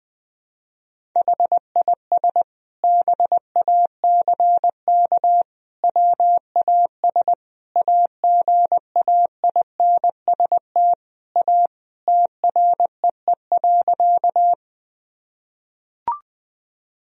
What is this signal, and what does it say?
Morse code, heard as a keyed tone, about 21 wpm, 700 Hz. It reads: HIS BACK WAS AGAINST A TREE.
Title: Sentences from Top 1000 Words - Review Part 1 - 20wpm